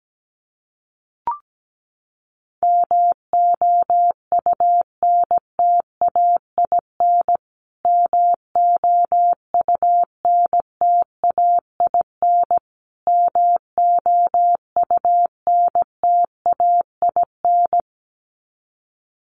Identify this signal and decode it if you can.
Morse code, heard as a keyed tone, about 17 wpm, 700 Hz. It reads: MOUNTAIN MOUNTAIN MOUNTAIN
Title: Top 500 Words Repeated 17wpm